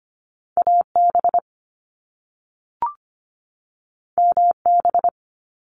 Morse code, heard as a keyed tone, about 25 wpm, 700 Hz. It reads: AB MB
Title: Canadian Provinces and Territories 25wpm